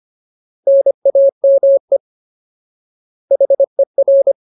Morse code, heard as a keyed tone, about 25 wpm, 550 Hz. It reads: NAME HER
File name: Ultimate Word List / 25WPM Ultimate